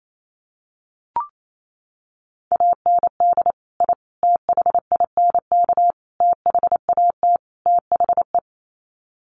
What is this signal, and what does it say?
Morse code, heard as a keyed tone, about 28 wpm, 700 Hz. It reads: AND I THINK THAT THE
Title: Sets of 5 Words 28wpm